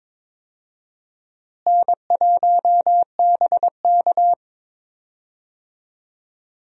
Morse code, heard as a keyed tone, about 22 wpm, 700 Hz. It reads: N1BK
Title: USA Call Signs - 4 Characters - Repeated 2x - 22wpm